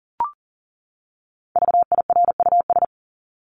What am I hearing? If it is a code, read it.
Morse code, heard as a keyed tone, about 40 wpm, 700 Hz. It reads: VIRUS
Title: Top 1251-1500 Words 40wpm